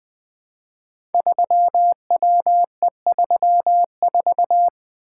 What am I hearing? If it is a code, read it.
Morse code, heard as a keyed tone, about 20 wpm, 700 Hz. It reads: 3WE34
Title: Code Groups - 5 Letter:Numbers - 20wpm